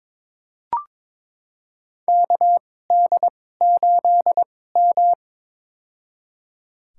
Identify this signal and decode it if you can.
Morse code, heard as a keyed tone, about 21 wpm, spaced as for 16 wpm, 700 Hz. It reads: KD8M
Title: USA Call Signs - 4 Characters - 2x Character Spacing - 22wpm